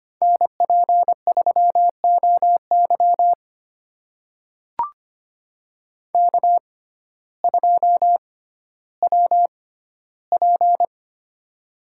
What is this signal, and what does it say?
Morse code, heard as a keyed tone, about 25 wpm, spaced as for 14 wpm, 700 Hz. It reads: NP3OY K2WP
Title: USA Call Signs - 6x Character Spacing - 25wpm